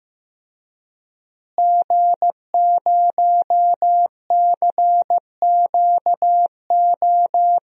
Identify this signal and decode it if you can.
Morse code, heard as a keyed tone, about 15 wpm, 700 Hz. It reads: G0CQO
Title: DXCC-100 Call Signs - 5 Characters - Repeated 2x - 15wpm